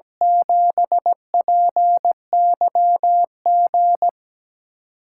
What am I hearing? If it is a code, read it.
Morse code, heard as a keyed tone, about 17 wpm, 700 Hz. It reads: 7PYG
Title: USA Call Signs - 6 Characters - 8x Character Spacing - 17wpm